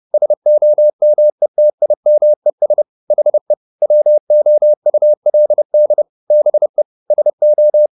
Morse code, heard as a keyed tone, about 30 wpm, 600 Hz. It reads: SOMETIMES HE WOULD BE SO
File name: the old man and the sea / the old man and the sea 1 30wpm